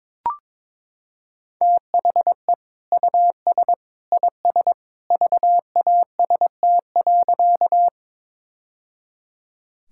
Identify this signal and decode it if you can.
Morse code, heard as a keyed tone, about 22 wpm, 700 Hz. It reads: THE US IS VAST.
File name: practice / 22wpm / Sentences with 2-Word Phrases - 22wpm